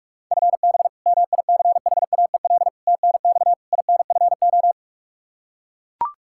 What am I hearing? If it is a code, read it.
Morse code, heard as a keyed tone, about 42 wpm, 700 Hz. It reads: FB MICHAEL TNX INFO
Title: Practice QSO - Exchange 3 - More Info - 45wpm